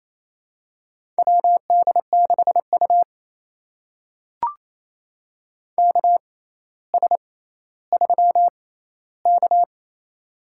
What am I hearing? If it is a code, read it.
Morse code, heard as a keyed tone, about 28 wpm, spaced as for 14 wpm, 700 Hz. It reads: WD6U KS3K
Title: USA Call Signs - 4 Characters - 6x Character Spacing - 28wpm